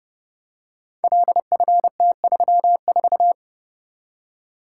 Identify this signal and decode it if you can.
Morse code, heard as a keyed tone, about 32 wpm, 700 Hz. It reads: LFT34
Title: Code Groups - 5 Letter:Numbers - 30wpm